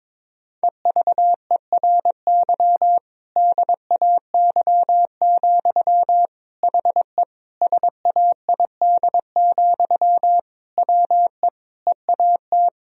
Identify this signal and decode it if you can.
Morse code, heard as a keyed tone, about 22 wpm, 700 Hz. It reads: EVERY DAY, HE SAID, WE EAT
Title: Sentences from Top 1000 Words - Review Part 1 - 22wpm